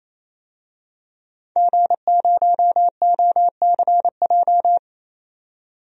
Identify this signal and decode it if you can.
Morse code, heard as a keyed tone, about 28 wpm, 700 Hz. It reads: G0OCJ
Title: DXCC-100 Call Signs - 5 Characters - 28wpm